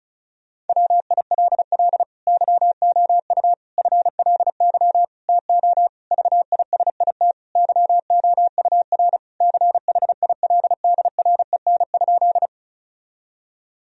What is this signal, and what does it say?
Morse code, heard as a keyed tone, about 36 wpm, 700 Hz. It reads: WILL YOU FLY TO VISIT YOUR CHILDREN?
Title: Sentences from Top 1000 Words - Review Part 2 - 35wpm